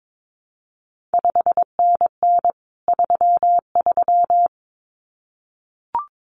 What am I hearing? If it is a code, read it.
Morse code, heard as a keyed tone, about 22 wpm, 700 Hz. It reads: 5NN 33
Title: CQ WW DX Contest Exchange Encore - 22wpm